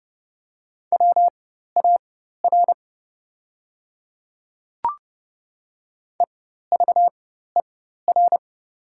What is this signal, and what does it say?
Morse code, heard as a keyed tone, about 30 wpm, spaced as for 14 wpm, 700 Hz. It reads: WAR EVER
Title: 4x Character Spacing - Top 500 Words - 30wpm